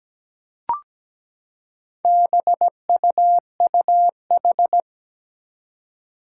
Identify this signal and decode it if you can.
Morse code, heard as a keyed tone, about 17 wpm, 700 Hz. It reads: BUUH
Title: Four Challenging Letter-Number Combinations 17wpm